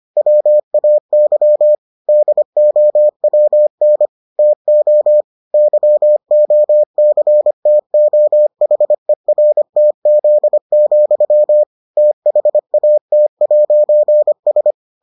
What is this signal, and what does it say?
Morse code, heard as a keyed tone, about 25 wpm, 600 Hz. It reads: WAY DOWN TO YOCTOHERTZ, THAT'S